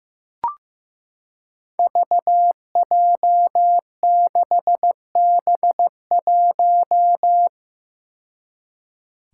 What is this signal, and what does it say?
Morse code, heard as a keyed tone, about 15 wpm, 700 Hz. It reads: VJ6B1
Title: Five Challenging Letter-Number Combinations 15wpm